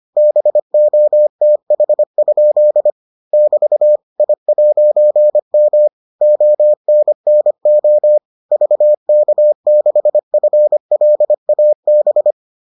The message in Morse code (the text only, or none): BOTH? <BT> I'M ONNO VK6FLAB